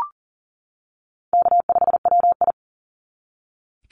Morse code, heard as a keyed tone, about 40 wpm, 700 Hz. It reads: K5WI
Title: NA SOTA Chaser Callsigns 40wpm